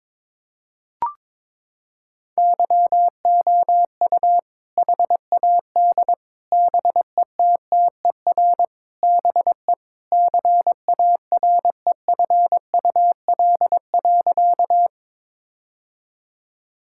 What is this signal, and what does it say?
Morse code, heard as a keyed tone, about 22 wpm, 700 Hz. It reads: YOU HAD BETTER BE CAREFUL.
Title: Sentences from Top 1600 Words 22wpm